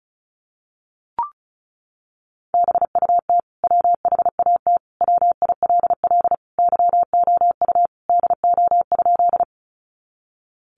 Morse code, heard as a keyed tone, about 34 wpm, 700 Hz. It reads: BUT WHAT WILL YOU DO?
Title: Sentences from Top 100 Words 35wpm